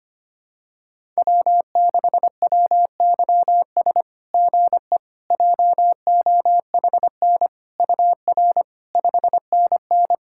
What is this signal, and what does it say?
Morse code, heard as a keyed tone, about 25 wpm, 690 Hz. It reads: W6WYS GE JOHN UR 5NN
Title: SOTA QSO Practice 25wpm